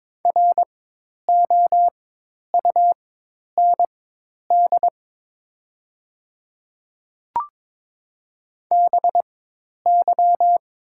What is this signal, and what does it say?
Morse code, heard as a keyed tone, about 22 wpm, spaced as for 10 wpm, 700 Hz. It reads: ROUND BY
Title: 4x Character Spacing - Top 300 Words - 22wpm